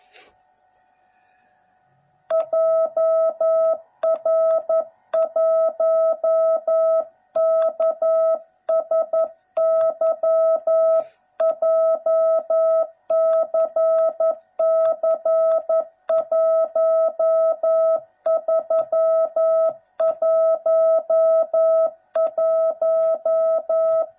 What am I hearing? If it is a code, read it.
Morse code, heard as a keyed tone, about 11 wpm, 660 Hz. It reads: JR1KSY JCC1311
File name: jh8cbh / 2021cw / cw09b